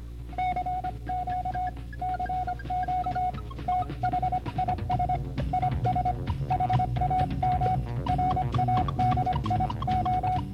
Morse code, heard as a keyed tone, about 25 wpm, 740 Hz, with noise. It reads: CQ CQ THIS IS HAM RADIO